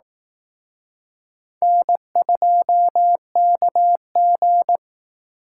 Morse code, heard as a keyed tone, about 18 wpm, 700 Hz. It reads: N2KG
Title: Speed Racing - USA 1x2 Call Signs 15wpm